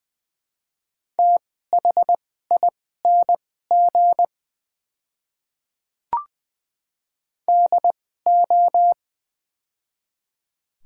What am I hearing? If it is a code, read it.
Morse code, heard as a keyed tone, about 20 wpm, spaced as for 14 wpm, 700 Hz. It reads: THING DO
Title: 2x Character Spacing - Top 100 Words - 20wpm